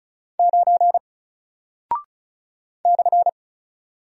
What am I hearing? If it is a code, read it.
Morse code, heard as a keyed tone, about 36 wpm, 700 Hz. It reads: 9 /
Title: Single Character (Letter:Number:Punctuation:Prosign) Alt 35wpm